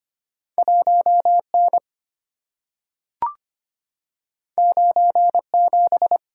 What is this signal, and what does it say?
Morse code, heard as a keyed tone, about 25 wpm, 700 Hz. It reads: 1N 97
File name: Two Letter-Number Combinations - 25wpm